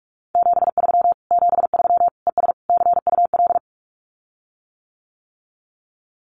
Morse code, heard as a keyed tone, about 44 wpm, 700 Hz. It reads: 73 73 ES CUL
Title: Common QSO Phrases Encore 45wpm